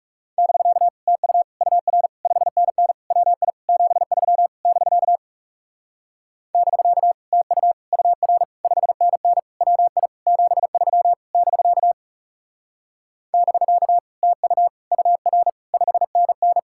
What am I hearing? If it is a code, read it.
Morse code, heard as a keyed tone, about 40 wpm, 700 Hz. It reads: <BK> TU UR 5NN WI 73 <BK> <BK> TU UR 5NN WI 73 <BK> <BK> TU UR 5NN